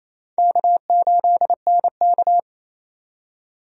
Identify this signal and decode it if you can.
Morse code, heard as a keyed tone, about 28 wpm, 700 Hz. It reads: K8NK
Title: USA Call Signs - 4 Characters - 28wpm